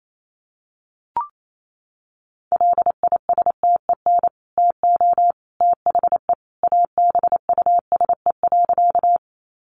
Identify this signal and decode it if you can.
Morse code, heard as a keyed tone, about 28 wpm, 700 Hz. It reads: LISTEN TO THE ABUSE.
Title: Sentences from Top 1300 Words 28wpm